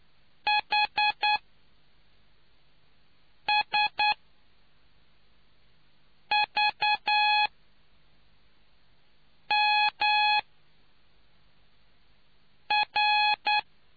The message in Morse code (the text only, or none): HSVMR